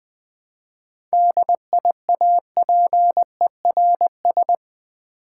Morse code, heard as a keyed tone, about 20 wpm, 700 Hz. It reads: DIAPERS
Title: Easy Words with Hints - 20wpm